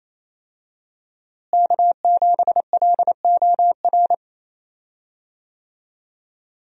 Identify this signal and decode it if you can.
Morse code, heard as a keyed tone, about 28 wpm, 700 Hz. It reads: K7LOR
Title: USA Call Signs - 5 Characters - Repeated 2x - 28wpm